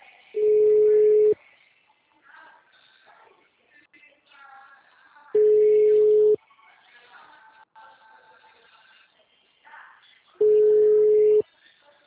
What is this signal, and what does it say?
Ringback tone (Europe, 425 Hz).